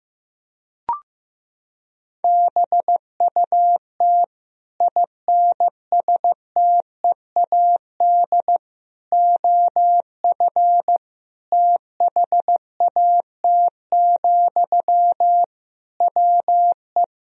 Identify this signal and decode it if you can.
Morse code, heard as a keyed tone, about 16 wpm, 700 Hz. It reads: BUT INSTEAD OF THAT, WE
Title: Sentences from Top 1000 Words - Review Part 4 - 15wpm